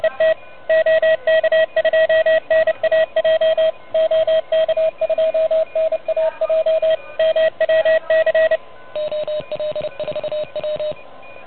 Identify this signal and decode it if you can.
Morse code, heard as a keyed tone, about 30 wpm, 590 Hz, with radio noise. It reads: ET OK2NAJ OK2NAJ MWC OL4W
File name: truSDX / MWC29.8